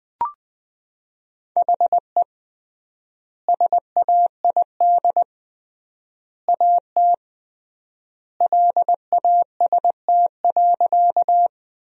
Morse code, heard as a keyed tone, about 20 wpm, 700 Hz. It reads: HE SAID AT LAST.
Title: 3x Word-Spacing - Sentences from Top 300 Words - 20wpm